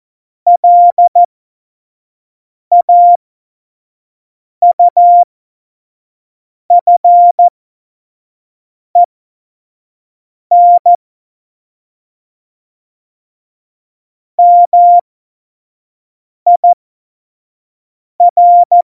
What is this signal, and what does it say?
Morse code, heard as a keyed tone, about 14 wpm, spaced as for 5 wpm, 700 Hz. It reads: LAUFEN MIR
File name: Morsekurs / Witze / Witz 02 Juni 2020, 25 F70